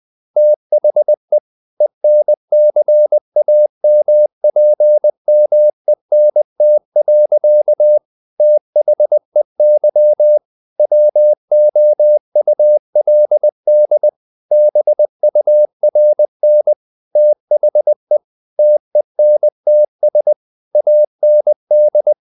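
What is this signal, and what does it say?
Morse code, heard as a keyed tone, about 20 wpm, 600 Hz. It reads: THE ENCAMPMENT. THEY WOULD BURN THE TENTS AND
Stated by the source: Morse code at 19 wpm from CW med Gnister, reading Ken Follett's novel Never in English.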